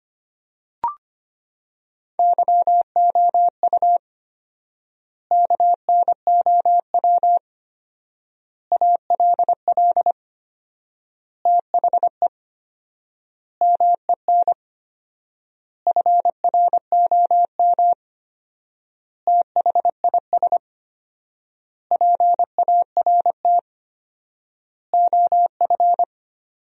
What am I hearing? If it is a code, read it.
Morse code, heard as a keyed tone, about 25 wpm, 700 Hz. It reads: YOU KNOW ALL THE MEN FROM THIS PART OF
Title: 4x Word-Spacing - Sentences from Top 300 Words - 25wpm